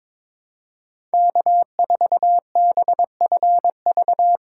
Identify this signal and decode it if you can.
Morse code, heard as a keyed tone, about 22 wpm, 700 Hz. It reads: K4BFV
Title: USA Callsign DE Callsign - 5 Characters - 22wpm